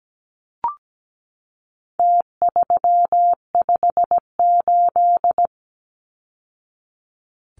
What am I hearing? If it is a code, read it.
Morse code, heard as a keyed tone, about 17 wpm, 700 Hz. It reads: T358
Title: Numbers - Contest Style - 4-Digit 17wpm